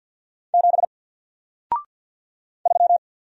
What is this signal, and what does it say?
Morse code, heard as a keyed tone, about 49 wpm, 700 Hz. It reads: 7 3
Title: Single Character (Letter:Number:Punctuation:Prosign) Alt 50wpm